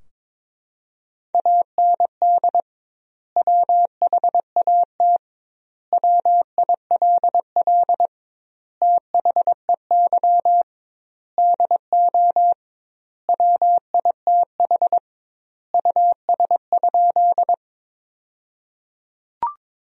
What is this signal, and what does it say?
Morse code, heard as a keyed tone, about 23 wpm, 700 Hz. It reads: AND WHAT WILL THEY DO WITH US?
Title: 2x Word-Spacing - Sentences from Top 200 Words - 22wpm